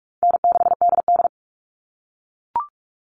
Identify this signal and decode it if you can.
Morse code, heard as a keyed tone, about 44 wpm, 700 Hz. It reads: N6DD
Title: All Cut Numbers - 4-Digit 45wpm